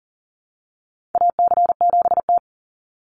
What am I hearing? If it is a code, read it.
Morse code, heard as a keyed tone, about 40 wpm, 700 Hz. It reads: AC7T
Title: Call Signs Repeated 2x - 40wpm